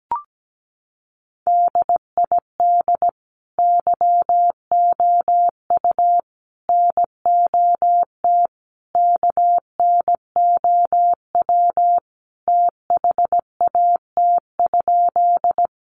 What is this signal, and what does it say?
Morse code, heard as a keyed tone, about 17 wpm, 700 Hz. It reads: DID YOU NOT KNOW THAT?
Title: Sentences from Top 600 Words 17wpm